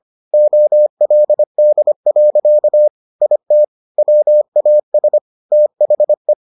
Morse code, heard as a keyed tone, about 25 wpm, 600 Hz. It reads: OLD. IT WAS THE